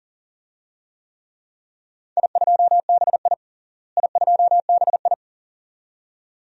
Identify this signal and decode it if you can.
Morse code, heard as a keyed tone, about 40 wpm, 700 Hz. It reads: I2BI I2BI
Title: DXCC-100 Call Signs - 4 Characters - Repeated 2x - 40wpm